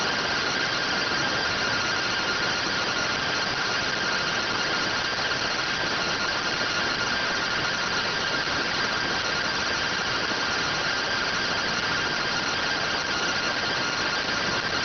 Two-tone teleprinter signal, about 170 Hz shift.